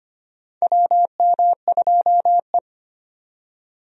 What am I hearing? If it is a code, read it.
Morse code, heard as a keyed tone, about 25 wpm, 700 Hz. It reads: WM2E
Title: USA Call Signs - 4 Characters - 25wpm